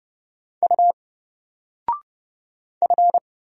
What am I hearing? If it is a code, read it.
Morse code, heard as a keyed tone, about 30 wpm, 700 Hz. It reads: U F
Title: Single Character (Letter:Number:Punctuation:Prosign) 30wpm